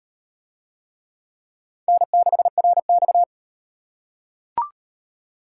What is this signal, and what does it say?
Morse code, heard as a keyed tone, about 38 wpm, 700 Hz. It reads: N6RX